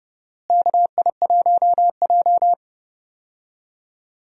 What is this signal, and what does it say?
Morse code, heard as a keyed tone, about 30 wpm, 700 Hz. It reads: KI1J
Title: USA Call Signs - 4 Characters - Repeated 2x - 30wpm